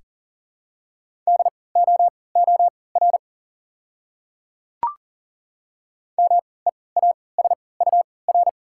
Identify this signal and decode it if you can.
Morse code, heard as a keyed tone, about 40 wpm, spaced as for 23 wpm, 690 Hz. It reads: DOOR MEASUR
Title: 3x Character Spacing - Top 500 Words - 40wpm